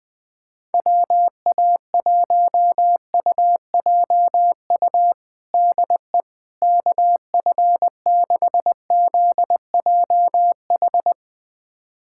Morse code, heard as a keyed tone, about 20 wpm, 700 Hz. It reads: WA1UJU DE KF6ZJH
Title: USA Callsign DE Callsign 20wpm